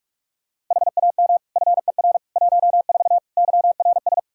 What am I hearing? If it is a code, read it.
Morse code, heard as a keyed tone, about 46 wpm, 700 Hz. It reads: HAM FER 14 YRS
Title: Practice QSO - Exchange 3 - More Info - 45wpm